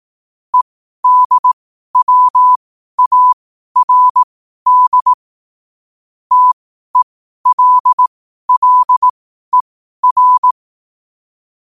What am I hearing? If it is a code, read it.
Morse code, heard as a keyed tone, about 18 wpm, spaced as for 12 wpm, 1000 Hz. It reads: EDWARD TELLER
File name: qotdfiles / 12wpm